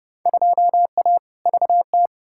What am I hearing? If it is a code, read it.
Morse code, heard as a keyed tone, about 30 wpm, 700 Hz. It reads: 2A VT
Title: Field Day Exchanges Encore - 30wpm